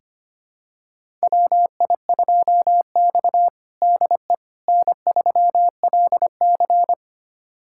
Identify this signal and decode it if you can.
Morse code, heard as a keyed tone, about 25 wpm, 700 Hz. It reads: WI2X DE N3LC
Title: USA Callsign DE Callsign - 4 Characters - 25wpm